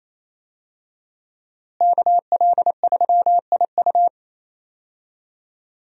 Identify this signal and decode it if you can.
Morse code, heard as a keyed tone, about 28 wpm, 700 Hz. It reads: KL3IU
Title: USA Call Signs - 5 Characters - Repeated 3x - 28wpm